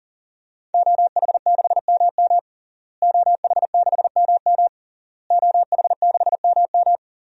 Morse code, heard as a keyed tone, about 40 wpm, 700 Hz. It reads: OH6MM OH6MM OH6MM
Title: DXCC-100 Call Signs - 5 Characters - Repeated 3x - 40wpm